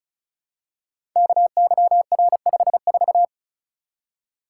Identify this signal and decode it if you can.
Morse code, heard as a keyed tone, about 36 wpm, 700 Hz. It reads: KYR54
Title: Code Groups - 5 Letter:Numbers - 35wpm